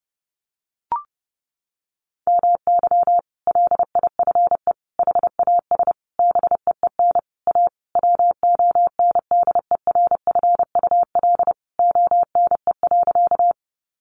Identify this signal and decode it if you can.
Morse code, heard as a keyed tone, about 30 wpm, 700 Hz. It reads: MY LIFE HAS BEEN A WONDERFUL ONE.
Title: Sentences from Top 1100 Words 30wpm